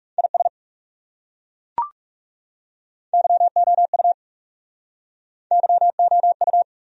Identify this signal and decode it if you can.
Morse code, heard as a keyed tone, about 40 wpm, 700 Hz. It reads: IS YOU YOU